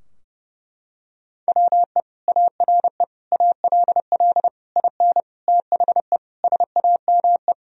Morse code, heard as a keyed tone, about 32 wpm, 700 Hz. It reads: WE ARE ALL IN THE SAME
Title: Sentences from Top 1000 Words - Review Part 3 - 30wpm